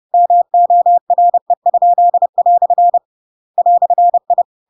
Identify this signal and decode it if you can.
Morse code, heard as a keyed tone, about 30 wpm, 700 Hz. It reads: MORE?" "I
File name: Pinocchio / Pinocchio ch8 30 30 ik1hge